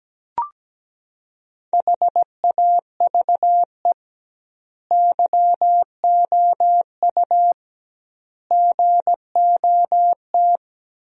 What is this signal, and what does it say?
Morse code, heard as a keyed tone, about 17 wpm, 700 Hz. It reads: HAVE YOU GOT
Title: 2x Word-Spacing - Sentences from Top 300 Words - 17wpm